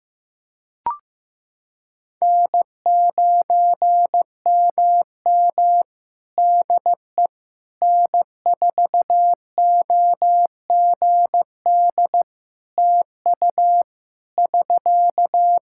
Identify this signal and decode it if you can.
Morse code, heard as a keyed tone, about 15 wpm, 700 Hz. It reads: N9MM DE N4OGD TU <SK>
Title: K1USN SST QSO Practice 15wpm